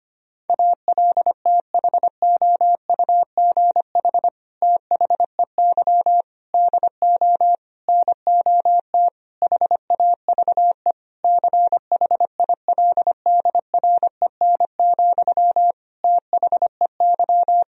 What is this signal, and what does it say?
Morse code, heard as a keyed tone, about 25 wpm, 700 Hz. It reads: ALTHOUGH THEY DO NOT HAVE CHILDREN, THEY
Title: Sentences from Top 1000 Words - Review Part 3 - 25wpm